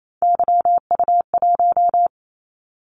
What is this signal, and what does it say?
Morse code, heard as a keyed tone, about 28 wpm, 700 Hz. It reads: YU1
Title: Callsign Prefixes for Top 25 DXCC - 28wpm